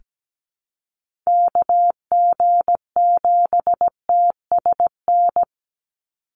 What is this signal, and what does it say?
Morse code, heard as a keyed tone, about 17 wpm, 700 Hz. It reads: KG7TSN
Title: USA Call Signs 17wpm